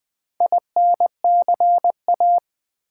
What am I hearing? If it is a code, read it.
Morse code, heard as a keyed tone, about 20 wpm, 700 Hz. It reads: INCA
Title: Build A Word - 4 Letters At A Time - 11-Letter Words - 20wpm